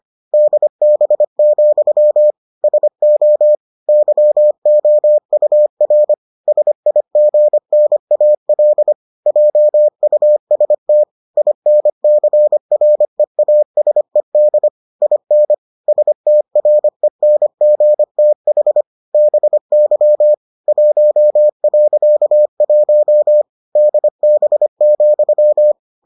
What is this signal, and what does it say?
Morse code, heard as a keyed tone, about 25 wpm, 600 Hz. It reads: DB, SO YOUR SIGNAL JUST INCREASED IN STRENGTH BY 1.1 DB,